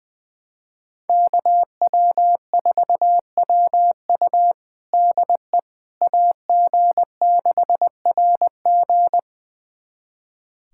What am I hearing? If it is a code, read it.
Morse code, heard as a keyed tone, about 21 wpm, 700 Hz. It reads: KW4WU DE AG6RG
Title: USA Callsign DE Callsign - 5 Characters - 20wpm